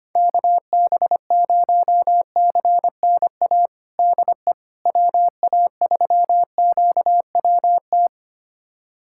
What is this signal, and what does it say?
Morse code, heard as a keyed tone, about 25 wpm, 700 Hz. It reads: KB0CNA DE WA3QWT
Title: USA Callsign DE Callsign - 6 Characters - 25wpm